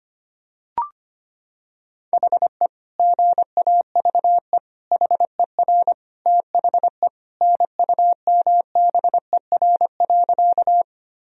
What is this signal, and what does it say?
Morse code, heard as a keyed tone, about 25 wpm, 700 Hz. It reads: HE GAVE HER THE NUMBER.
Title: Sentences from Top 1000 Words - Review Part 1 - 25wpm